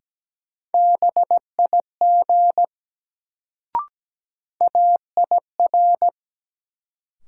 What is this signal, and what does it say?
Morse code, heard as a keyed tone, about 17 wpm, 700 Hz. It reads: BIG AIR
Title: Sentences from Top 200 Words - New Word Drill - 17wpm